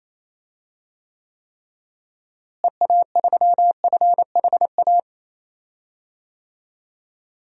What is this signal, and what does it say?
Morse code, heard as a keyed tone, about 28 wpm, 700 Hz. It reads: EA3FHA
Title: DXCC-100 Call Signs - 6 Characters - Repeated 3x - 28wpm